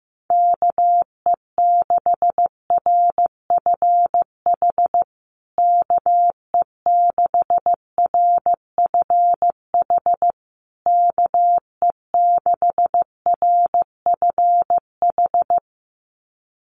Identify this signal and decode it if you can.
Morse code, heard as a keyed tone, about 15 wpm, 700 Hz. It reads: KE6RFH KE6RFH KE6RFH
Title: USA Call Signs Repeated 3x 15wpm